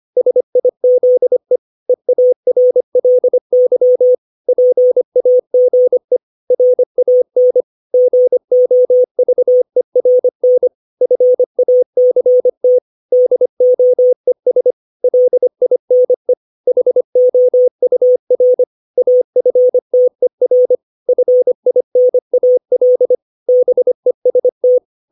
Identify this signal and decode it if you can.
Morse code, heard as a keyed tone, about 25 wpm, 500 Hz. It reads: SIZE EARLY PAGE RAN GOVERN FACT DOES LINE HOUR AFTER FINAL BEST
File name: 632 words / 632 Words 25WPM